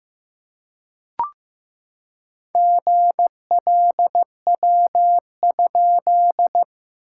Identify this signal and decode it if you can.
Morse code, heard as a keyed tone, about 15 wpm, 700 Hz. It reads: GLW?
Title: SOTA QSO Practice 15wpm